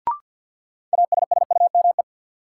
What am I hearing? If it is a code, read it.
Morse code, heard as a keyed tone, about 53 wpm, 700 Hz. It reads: ASSUME
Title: Sentences from Top 1500 Words - New Word Drill - 50wpm